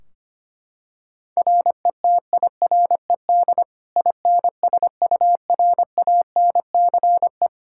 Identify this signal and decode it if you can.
Morse code, heard as a keyed tone, about 25 wpm, 700 Hz. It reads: RETIRED INSURANCE
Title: Practice QSO - Exchange 3 - More Info - 25wpm